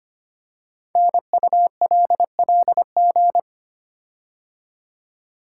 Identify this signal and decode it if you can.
Morse code, heard as a keyed tone, about 25 wpm, 700 Hz. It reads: NULLG
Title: Code Groups - 5 Letter:Numbers - 25wpm